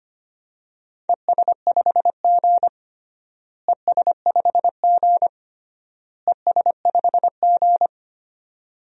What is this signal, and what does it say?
Morse code, heard as a keyed tone, about 25 wpm, 700 Hz. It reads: ES5G ES5G ES5G
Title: DXCC-100 Call Signs - 4 Characters - Repeated 3x - 25wpm